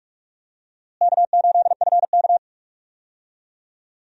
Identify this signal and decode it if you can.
Morse code, heard as a keyed tone, about 46 wpm, 700 Hz. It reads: K8FK
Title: USA Call Signs - 4 Characters - Repeated 2x - 45wpm